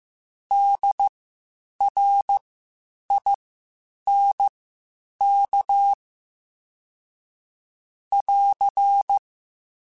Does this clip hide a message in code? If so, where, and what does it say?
Yes: Morse code from 0.5 to 9.2 s, heard as a keyed tone, about 15 wpm, spaced as for 8 wpm, 800 Hz. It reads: DRINK <AR>